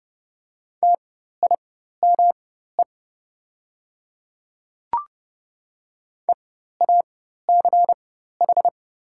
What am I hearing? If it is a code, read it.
Morse code, heard as a keyed tone, about 30 wpm, spaced as for 14 wpm, 690 Hz. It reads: TIME EACH
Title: 4x Character Spacing - Top 100 Words - 30wpm